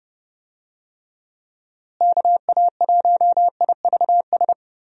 Morse code, heard as a keyed tone, about 31 wpm, 700 Hz. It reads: KA1IVS
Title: USA Call Signs - 6 Characters - Repeated 2x - 30wpm